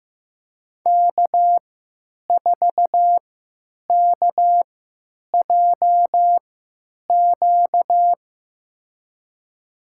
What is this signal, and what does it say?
Morse code, heard as a keyed tone, about 15 wpm, spaced as for 9 wpm, 700 Hz. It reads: K4KJQ
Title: USA Call Signs - 5 Characters - 3x Character Spacing - 15wpm